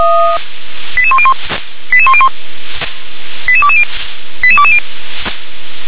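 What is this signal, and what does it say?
Five-tone selective call (ZVEI), heard as quick runs of five tones, tones of about 70 ms. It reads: 88101 88101 88220 88220
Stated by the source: ZVEI five-tone selective call 88051 received by SDR.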